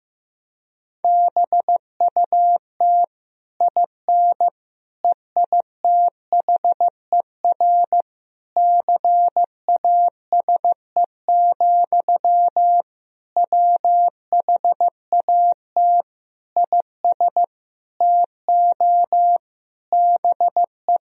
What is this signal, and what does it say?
Morse code, heard as a keyed tone, about 15 wpm, 700 Hz. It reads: BUT IN EITHER CASE, WHAT IS TO BE
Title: Sentences from Top 1000 Words - Review Part 4 - 15wpm